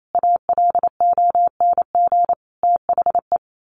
Morse code, heard as a keyed tone, about 28 wpm, 700 Hz. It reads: ALONG THE
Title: A Sound Of Thunder 28wpm